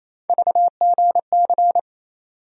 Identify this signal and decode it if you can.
Morse code, heard as a keyed tone, about 28 wpm, 700 Hz. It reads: VGC
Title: USA Call Sign Postfixes - 3 Letters 28wpm